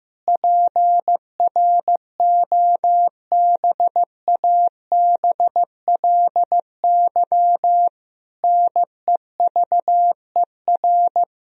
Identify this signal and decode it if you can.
Morse code, heard as a keyed tone, about 16 wpm, 700 Hz. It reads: PROBABLY NEVER
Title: Sentences from Top 1000 Words - Review Part 4 - 15wpm